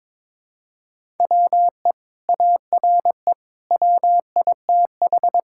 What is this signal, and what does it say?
Morse code, heard as a keyed tone, about 22 wpm, 700 Hz. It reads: WE ARE WITH
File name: practice / 22wpm / Sentences with 2-Word Phrases - 22wpm